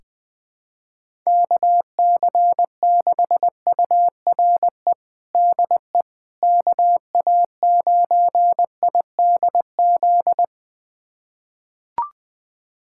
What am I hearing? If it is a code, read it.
Morse code, heard as a keyed tone, about 21 wpm, 700 Hz. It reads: KC6URE DE KA9IDZ
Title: USA Callsign DE Callsign - 6 Characters - 20wpm